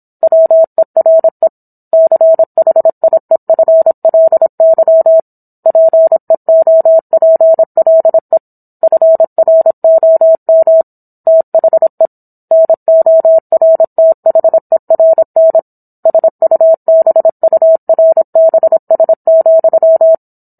Morse code, heard as a keyed tone, about 27 wpm, 650 Hz. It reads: WERE CHIEFLY PEOPLE FROM THE NORTHERN SUBURBS,